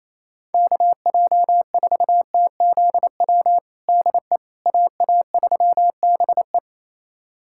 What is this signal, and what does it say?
Morse code, heard as a keyed tone, about 28 wpm, 700 Hz. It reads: KJ4TZW DE AA3BE
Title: USA Callsign DE Callsign 28wpm